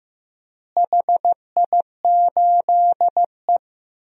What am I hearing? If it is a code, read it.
Morse code, heard as a keyed tone, about 16 wpm, 700 Hz. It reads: HI8E
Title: DXCC-100 Call Signs - 4 Characters - Repeated 2x - 15wpm